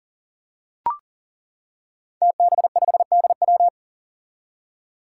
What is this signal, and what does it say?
Morse code, heard as a keyed tone, about 38 wpm, 700 Hz. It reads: TB5DW
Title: Five Challenging Letter-Number Combinations 40wpm